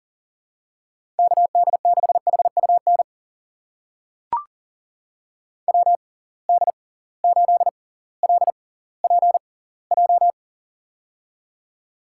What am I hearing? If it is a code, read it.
Morse code, heard as a keyed tone, about 40 wpm, spaced as for 20 wpm, 700 Hz. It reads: KD6HUN WD8LPJ
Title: USA Call Signs - 6x Character Spacing - 40wpm